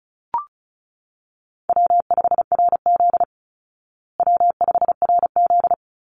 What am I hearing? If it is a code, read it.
Morse code, heard as a keyed tone, about 34 wpm, 700 Hz. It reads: W5RZ W5RZ
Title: Call Signs Repeated 3x - 35wpm